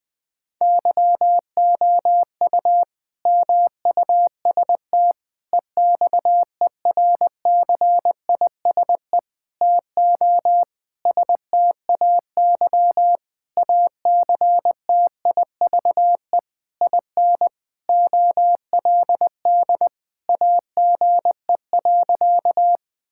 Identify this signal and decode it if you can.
Morse code, heard as a keyed tone, about 20 wpm, 700 Hz. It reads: YOU MUST EXERCISE TO STAY ACTIVE IN OLD AGE.
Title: Sentences from Top 1000 Words - Review Part 6 - 20wpm